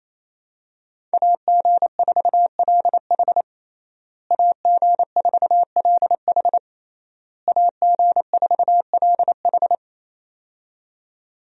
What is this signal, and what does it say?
Morse code, heard as a keyed tone, about 28 wpm, 700 Hz. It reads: AG4LH AG4LH AG4LH
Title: USA Call Signs - 5 Characters - Repeated 3x - 28wpm